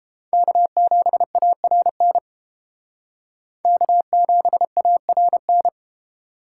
Morse code, heard as a keyed tone, about 31 wpm, 700 Hz. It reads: K7ARN K7ARN